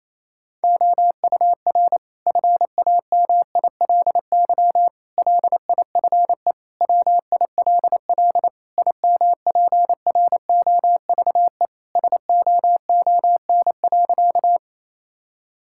Morse code, heard as a keyed tone, about 28 wpm, 700 Hz. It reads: OUR FAMILY LIFE WILL IMPROVE SOON.
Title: Sentences from Top 1000 Words - Review Part 5 - 28wpm